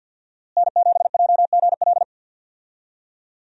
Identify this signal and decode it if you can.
Morse code, heard as a keyed tone, about 46 wpm, 700 Hz. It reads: N7JGL
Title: USA Call Signs - 5 Characters - Repeated 3x - 50wpm